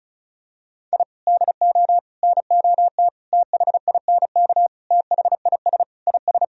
Morse code, heard as a keyed tone, about 34 wpm, 700 Hz. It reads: I DO NOT THINK THIS IS
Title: Sentences from Top 1000 Words - Review Part 4 - 35wpm